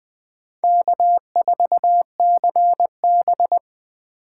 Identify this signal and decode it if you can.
Morse code, heard as a keyed tone, about 20 wpm, 700 Hz. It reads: K4CB
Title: USA Call Signs - 8x Character Spacing - 20wpm